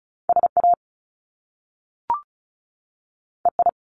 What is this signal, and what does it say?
Morse code, heard as a keyed tone, about 36 wpm, 700 Hz. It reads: SA EI
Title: Top Two Letter Combinations in Words 35wpm